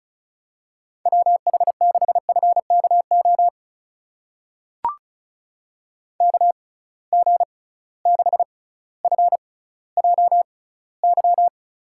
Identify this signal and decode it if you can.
Morse code, heard as a keyed tone, about 36 wpm, spaced as for 18 wpm, 700 Hz. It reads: WH6FKO KG6FJY
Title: USA Call Signs - 6 Characters - 6x Character Spacing - 35wpm